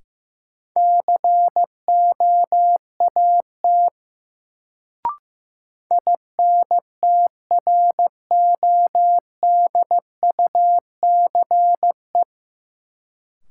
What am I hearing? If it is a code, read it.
Morse code, heard as a keyed tone, about 15 wpm, 700 Hz. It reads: COAT INTRODUCE
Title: Sentences from Top 1600 Words - New Word Drill Alt - 15wpm